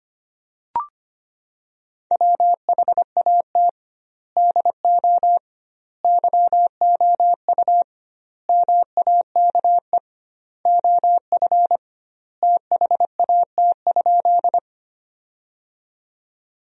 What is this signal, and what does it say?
Morse code, heard as a keyed tone, about 25 wpm, 700 Hz. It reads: WHAT DO YOU MAKE OF THAT?
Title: 2x Word-Spacing - Sentences from Top 100 Words - 25wpm